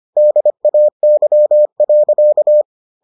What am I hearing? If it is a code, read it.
Morse code, heard as a keyed tone, about 25 wpm, 600 Hz. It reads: DAY.